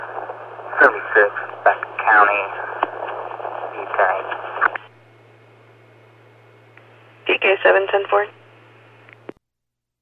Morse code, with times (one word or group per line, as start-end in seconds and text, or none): none